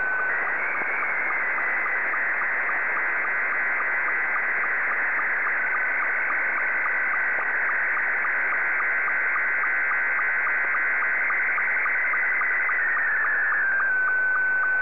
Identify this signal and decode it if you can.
SSTV (slow-scan television), Scottie 2.